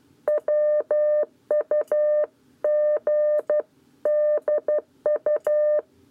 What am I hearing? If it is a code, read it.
Morse code, heard as a keyed tone, about 12 wpm, 600 Hz. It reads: WUGDU
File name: 7b lezione cw 1